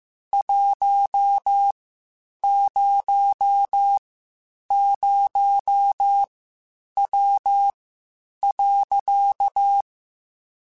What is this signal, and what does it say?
Morse code, heard as a keyed tone, about 15 wpm, spaced as for 8 wpm, 800 Hz. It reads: 100W.
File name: QSOs / QSOs-mp3s / WD0FBT 1@13